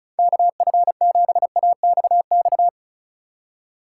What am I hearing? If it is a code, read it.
Morse code, heard as a keyed tone, about 34 wpm, 700 Hz. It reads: KF7AXX
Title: USA Call Signs - 6 Characters - 2x Character Spacing - 35wpm